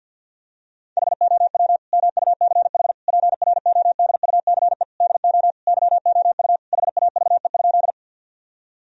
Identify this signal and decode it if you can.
Morse code, heard as a keyed tone, about 51 wpm, 700 Hz. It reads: HOW MUCH PRODUCE DO YOU HAVE?
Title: Sentences from Top 1000 Words - Review Part 3 - 50wpm